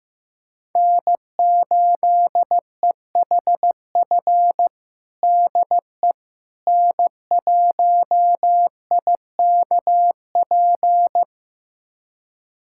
Morse code, heard as a keyed tone, about 15 wpm, 700 Hz. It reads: N8EHF DE N1IKP
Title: USA Callsign DE Callsign - 5 Characters - 15wpm